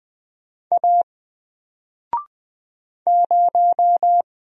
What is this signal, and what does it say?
Morse code, heard as a keyed tone, about 20 wpm, 700 Hz. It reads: A 0
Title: Single Letter-Number Alt 20wpm